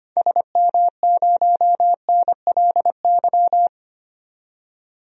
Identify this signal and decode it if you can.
Morse code, heard as a keyed tone, about 25 wpm, 700 Hz. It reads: SM0NLY
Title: DXCC-100 Call Signs - 6 Characters - 25wpm